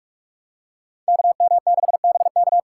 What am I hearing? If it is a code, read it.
Morse code, heard as a keyed tone, about 46 wpm, 700 Hz. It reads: KM6BK
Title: USA Call Signs - 5 Characters - 4x Character Spacing - 45wpm